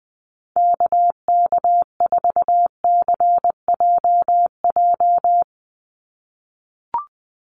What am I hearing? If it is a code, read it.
Morse code, heard as a keyed tone, about 20 wpm, 700 Hz. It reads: KK4CJJ
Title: USA Call Signs 20wpm